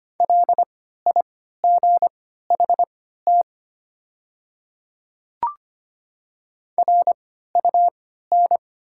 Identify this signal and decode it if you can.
Morse code, heard as a keyed tone, about 25 wpm, spaced as for 14 wpm, 700 Hz. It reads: LIGHT RUN
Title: 3x Character Spacing - Top 300 Words - 25wpm